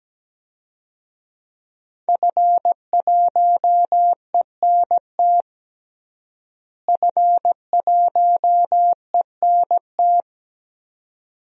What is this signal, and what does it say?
Morse code, heard as a keyed tone, about 17 wpm, 700 Hz. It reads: F1ENT F1ENT
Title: DXCC-100 Call Signs - 5 Characters - Repeated 2x - 17wpm